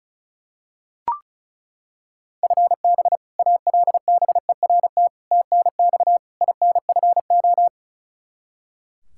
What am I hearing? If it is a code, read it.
Morse code, heard as a keyed tone, about 36 wpm, 700 Hz. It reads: FB ALBERT TNX INFO
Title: Practice QSO - Exchange 3 - More Info - 35wpm